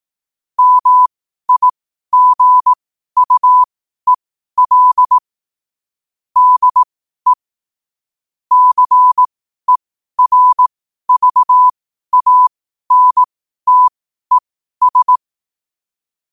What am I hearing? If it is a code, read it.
Morse code, heard as a keyed tone, about 18 wpm, spaced as for 12 wpm, 1000 Hz. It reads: MIGUEL DE CERVANTES
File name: qotdfiles / 12wpm